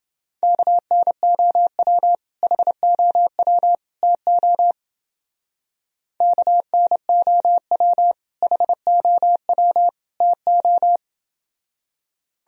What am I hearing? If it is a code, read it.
Morse code, heard as a keyed tone, about 28 wpm, 700 Hz. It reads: KNOW HOW TO KNOW HOW TO